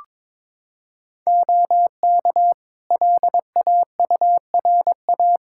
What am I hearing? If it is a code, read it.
Morse code, heard as a keyed tone, about 22 wpm, 700 Hz. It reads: OK LAURA
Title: K1USN SST QSO Practice 22wpm